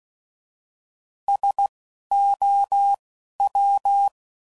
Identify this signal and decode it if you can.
Morse code, heard as a keyed tone, about 16 wpm, spaced as for 12 wpm, 800 Hz. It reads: SOW